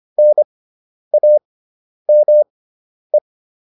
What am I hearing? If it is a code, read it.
Morse code, heard as a keyed tone, about 25 wpm, spaced as for 10 wpm, 600 Hz. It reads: NAME